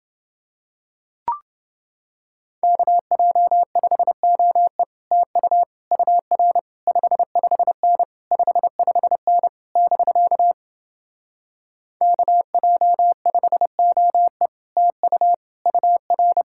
Morse code, heard as a keyed tone, about 30 wpm, 700 Hz. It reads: KJ5OE TU UR 55N 55N <BK> KJ5OE TU UR